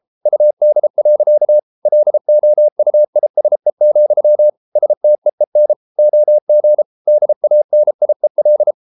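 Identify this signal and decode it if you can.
Morse code, heard as a keyed tone, about 33 wpm, 600 Hz. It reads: UD. LOUISE, STEEN OG DANIEL
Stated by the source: Morse code at 33 wpm, QRQ.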